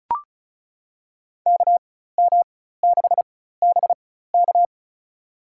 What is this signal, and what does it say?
Morse code, heard as a keyed tone, about 36 wpm, spaced as for 16 wpm, 700 Hz. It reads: KM6BK
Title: USA Call Signs - 5 Characters - 4x Character Spacing - 35wpm